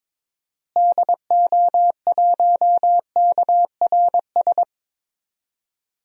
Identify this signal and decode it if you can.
Morse code, heard as a keyed tone, about 22 wpm, 700 Hz. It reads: DO1KRS
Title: DXCC-100 Call Signs - 6 Characters - Repeated 3x - 22wpm